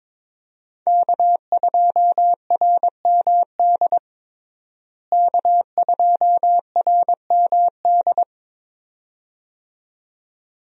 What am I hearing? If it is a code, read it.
Morse code, heard as a keyed tone, about 22 wpm, 700 Hz. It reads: K2RMD K2RMD
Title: USA Call Signs - 5 Characters - Repeated 2x - 22wpm